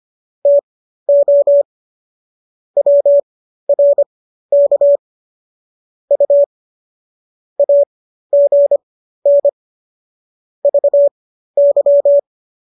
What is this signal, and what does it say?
Morse code, heard as a keyed tone, about 25 wpm, spaced as for 13 wpm, 570 Hz. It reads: TO WRK U AGN VY